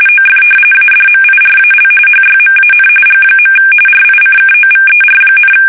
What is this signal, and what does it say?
RTTY, about 45.45 baud. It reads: OF ADMIRAL HYMAN G. RICKOVER,
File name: fsk850 / usnavy5